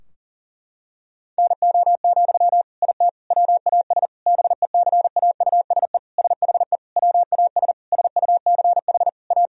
Morse code, heard as a keyed tone, about 40 wpm, 700 Hz. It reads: NO, IT WAS BECAUSE SHE WAS SUCH A
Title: Sentences from Top 1000 Words - Review Part 4 - 40wpm